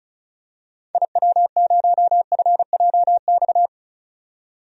Morse code, heard as a keyed tone, about 34 wpm, 700 Hz. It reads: IW0FJX
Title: DXCC-100 Call Signs - 6 Characters - 35wpm